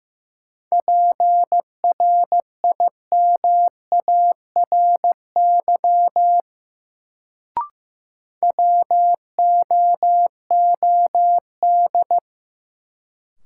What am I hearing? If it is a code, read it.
Morse code, heard as a keyed tone, about 15 wpm, 700 Hz. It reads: PRIMARY WOOD
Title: Sentences from Top 1100 Words - New Word Drill Alt - 15wpm